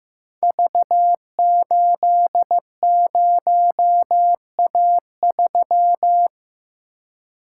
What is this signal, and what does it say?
Morse code, heard as a keyed tone, about 15 wpm, 700 Hz. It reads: V80A3
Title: Code Groups - 5 Letter:Numbers - 15wpm